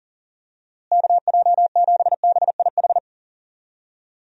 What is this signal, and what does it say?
Morse code, heard as a keyed tone, about 40 wpm, 700 Hz. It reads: KJ7BIH
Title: USA Call Signs - 6 Characters - Repeated 3x - 40wpm